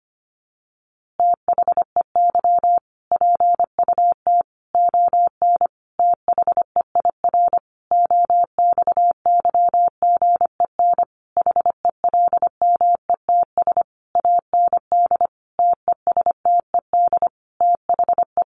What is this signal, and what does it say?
Morse code, heard as a keyed tone, about 25 wpm, 700 Hz. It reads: THEY PUT ON THEIR OXYGEN HELMETS AND TESTED THE
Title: A Sound Of Thunder 25wpm